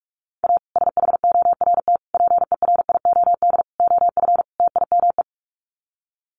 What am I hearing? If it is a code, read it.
Morse code, heard as a keyed tone, about 44 wpm, 700 Hz. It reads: A SHORT PERIOD OF TIME
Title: Sets of 5 Words Encore 45wpm